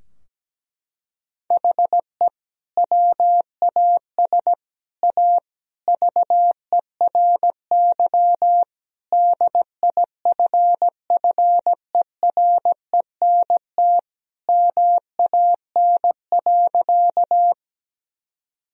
Morse code, heard as a keyed tone, about 17 wpm, 700 Hz. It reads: HE WAS A VERY DIFFERENT MAN.
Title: Sentences from Top 1000 Words - Review Part 3 - 17wpm